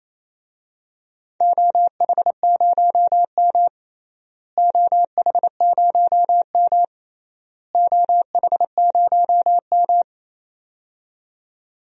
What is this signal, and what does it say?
Morse code, heard as a keyed tone, about 27 wpm, 700 Hz. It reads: OH0M OH0M OH0M
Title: DXCC-100 Call Signs - 4 Characters - Repeated 3x - 28wpm